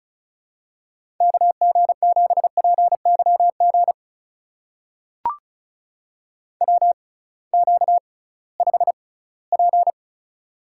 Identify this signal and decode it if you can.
Morse code, heard as a keyed tone, about 36 wpm, spaced as for 21 wpm, 700 Hz. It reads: KG7PYG WQ5P
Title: USA Call Signs - 6x Character Spacing - 35wpm